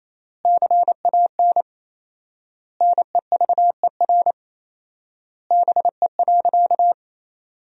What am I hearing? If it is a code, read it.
Morse code, heard as a keyed tone, about 28 wpm, 700 Hz. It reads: CAN NEVER BE.
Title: 4x Word-Spacing - Sentences from Top 300 Words - 28wpm